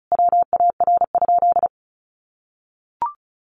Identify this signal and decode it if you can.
Morse code, heard as a keyed tone, about 34 wpm, 700 Hz. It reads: WAR?
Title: Sentences from Top 1400 Words 35wpm